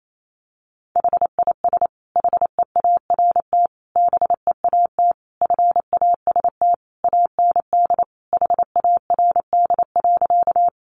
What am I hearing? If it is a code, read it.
Morse code, heard as a keyed tone, about 28 wpm, 700 Hz. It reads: HIS HEART BEAT FAST AND HARD.
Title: Sentences from Top 1300 Words 28wpm